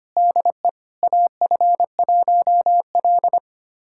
Morse code, heard as a keyed tone, about 25 wpm, 700 Hz. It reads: DE AF1L
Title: USA Callsign DE Callsign - 4 Characters - 25wpm